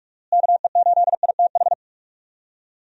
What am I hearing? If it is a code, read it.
Morse code, heard as a keyed tone, about 46 wpm, 700 Hz. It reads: KE8ITH
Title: USA Call Signs - 6 Characters - Repeated 2x - 45wpm